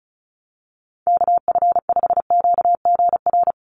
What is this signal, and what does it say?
Morse code, heard as a keyed tone, about 34 wpm, 700 Hz. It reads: KF5QGR
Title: USA Call Signs Repeated 3x 35wpm